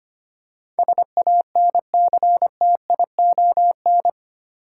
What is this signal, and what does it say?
Morse code, heard as a keyed tone, about 25 wpm, 700 Hz. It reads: SANCTION
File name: 8 Letter Words - First 3 Letters Given - 25wpm